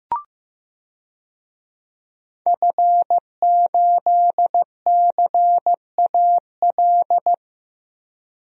Morse code, heard as a keyed tone, about 15 wpm, 700 Hz. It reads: F8CAL
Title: DXCC-100 Call Signs - 5 Characters - 15wpm